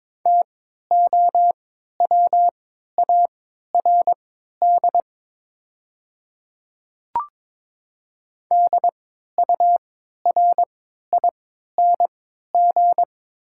Morse code, heard as a keyed tone, about 22 wpm, spaced as for 12 wpm, 700 Hz. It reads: TOWARD DURING
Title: 3x Character Spacing - Top 500 Words - 22wpm